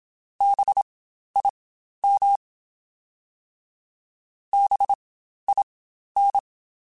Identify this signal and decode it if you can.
Morse code, heard as a keyed tone, about 27 wpm, spaced as for 12 wpm, 800 Hz. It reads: BIM BIN
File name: Spaced Alphas / Spaced Alphas mp3s / Bs sp @25